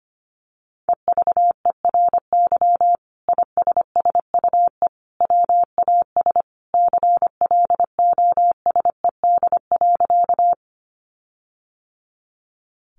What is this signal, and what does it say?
Morse code, heard as a keyed tone, about 25 wpm, 700 Hz. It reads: EVERY ISSUE WAS CLOSED.
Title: Sentences from Top 1400 Words 25wpm